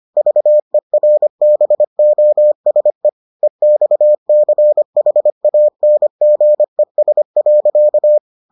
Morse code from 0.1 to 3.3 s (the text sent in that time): VERBOSE